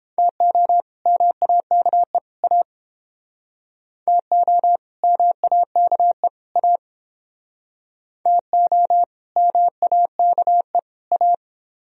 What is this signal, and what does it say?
Morse code, heard as a keyed tone, about 30 wpm, 700 Hz. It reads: TO MAKE A TO MAKE A TO MAKE A